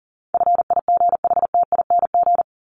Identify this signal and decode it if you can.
Morse code, heard as a keyed tone, about 40 wpm, 700 Hz. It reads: FIGHTING
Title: ING Suffix - Top 100 Words with Base Words 40wpm